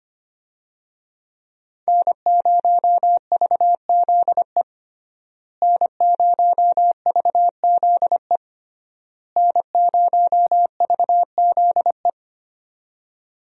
Morse code, heard as a keyed tone, about 25 wpm, 700 Hz. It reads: N0VZE N0VZE N0VZE
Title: USA Call Signs - 5 Characters - Repeated 3x - 25wpm